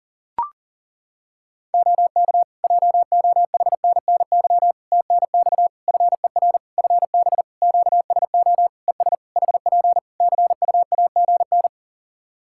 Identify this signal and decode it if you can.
Morse code, heard as a keyed tone, about 40 wpm, 700 Hz. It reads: OK JOHNNY TNX FER FB QSO ES HP CUAGN
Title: Practice QSO - Exchange 4 - Ending - 40wpm